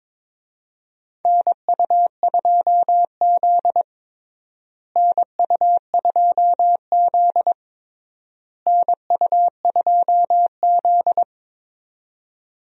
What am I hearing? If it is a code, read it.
Morse code, heard as a keyed tone, about 22 wpm, 700 Hz. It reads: NU2Z NU2Z NU2Z
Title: USA Call Signs - 4 Characters - Repeated 3x - 22wpm